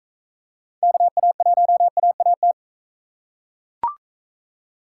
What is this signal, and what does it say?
Morse code, heard as a keyed tone, about 40 wpm, 700 Hz. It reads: KA1AAT